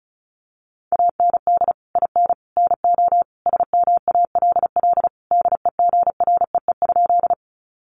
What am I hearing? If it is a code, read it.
Morse code, heard as a keyed tone, about 34 wpm, 700 Hz. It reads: AND IN NO SMALL DEGREE?
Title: Sentences from Top 1500 Words 35wpm